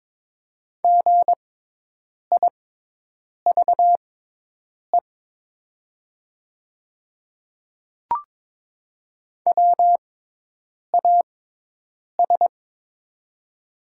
Morse code, heard as a keyed tone, about 22 wpm, spaced as for 8 wpm, 700 Hz. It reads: GIVE WAS
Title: 6x Character Spacing - Top 200 Words - 22wpm